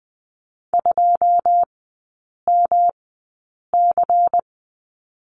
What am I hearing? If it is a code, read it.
Morse code, heard as a keyed tone, about 20 wpm, spaced as for 8 wpm, 700 Hz. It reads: 2MC
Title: 218 Lesson 35e - Number 0 - Three Characters 20wpm